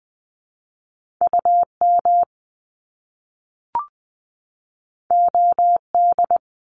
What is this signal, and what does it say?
Morse code, heard as a keyed tone, about 19 wpm, 700 Hz. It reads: UM OD
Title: Top Two Letter Combinations in Words 20wpm